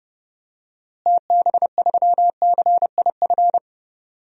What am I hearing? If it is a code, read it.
Morse code, heard as a keyed tone, about 30 wpm, 700 Hz. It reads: TB3CIF
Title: DXCC-100 Call Signs - 6 Characters - Repeated 2x - 30wpm